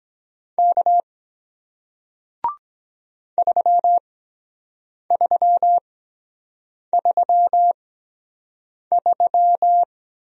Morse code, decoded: K 3333

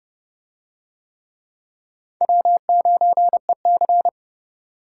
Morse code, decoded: W9EC

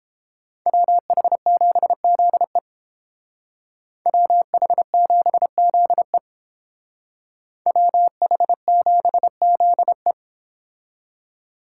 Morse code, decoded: WH7ZE WH7ZE WH7ZE

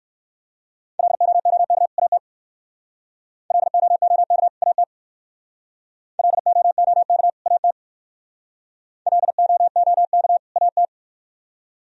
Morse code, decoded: LOOK AT LOOK AT LOOK AT LOOK AT

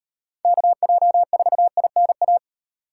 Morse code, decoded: KJ4INA

